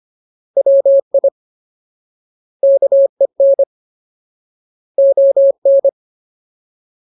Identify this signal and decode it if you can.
Morse code, heard as a keyed tone, about 25 wpm, 550 Hz. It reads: WI KEN ON
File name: Ultimate Word List / 25WPM Ultimate